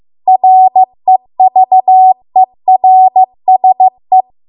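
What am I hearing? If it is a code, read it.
Morse code, heard as a keyed tone, about 15 wpm, 750 Hz. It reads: REVERSE
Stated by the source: Morse code practice at 15 wpm from the national amateur radio association's code archive.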